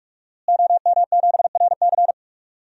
Morse code, heard as a keyed tone, about 46 wpm, 700 Hz. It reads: OM7RC
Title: DXCC-100 Call Signs - 5 Characters - Repeated 2x - 45wpm